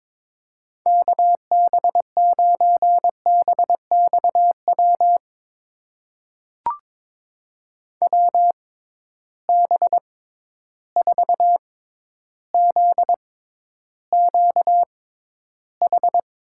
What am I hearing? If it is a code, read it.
Morse code, heard as a keyed tone, about 22 wpm, spaced as for 11 wpm, 700 Hz. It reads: KB9BXW WB4ZQH
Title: USA Call Signs - 6 Characters - 6x Character Spacing - 22wpm